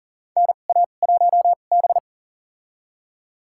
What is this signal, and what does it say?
Morse code, heard as a keyed tone, about 37 wpm, spaced as for 28 wpm, 700 Hz. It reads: NA1B
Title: USA Call Signs - 4 Characters - 2x Character Spacing - 40wpm